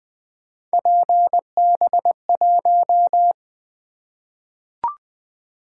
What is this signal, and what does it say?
Morse code, heard as a keyed tone, about 20 wpm, 700 Hz. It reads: PB1
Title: Three Letter-Number Combinations - 20wpm